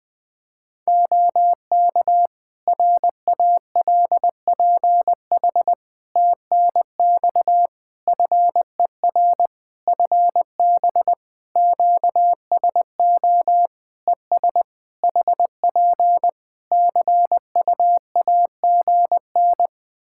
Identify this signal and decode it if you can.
Morse code, heard as a keyed tone, about 20 wpm, 700 Hz. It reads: OK RALPH TNX FER FB QSO ES HP CU AGN
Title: Practice QSO - Exchange 4 - Ending - 20wpm